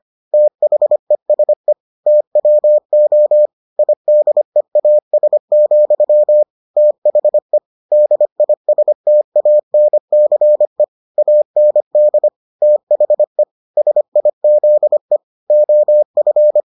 Morse code, heard as a keyed tone, about 25 wpm, 600 Hz. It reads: THESE TWO IDEAS, THE DISTANCE AND THE SIZE OF